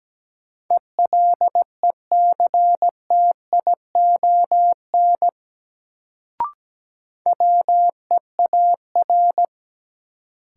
Morse code, heard as a keyed tone, about 17 wpm, 700 Hz. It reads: ELECTION WEAR
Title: Sentences from Top 1500 Words - New Word Drill Alt - 17wpm